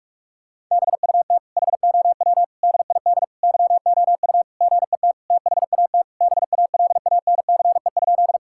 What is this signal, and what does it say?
Morse code, heard as a keyed tone, about 46 wpm, 700 Hz. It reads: BUT HOW DID YOU GET THAT BALANCE?
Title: Sentences from Top 1000 Words - Review Part 6 - 45wpm